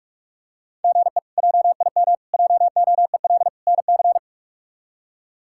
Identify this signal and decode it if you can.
Morse code, heard as a keyed tone, about 46 wpm, 700 Hz. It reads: GE JIM JOEL NC
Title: K1USN SST QSO Practice - 45wpm